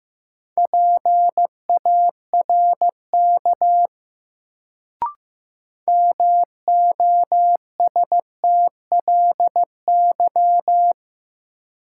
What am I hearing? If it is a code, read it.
Morse code, heard as a keyed tone, about 15 wpm, 700 Hz. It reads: PARK MOSTLY
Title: Sentences from Top 1300 Words - New Word Drill - 15wpm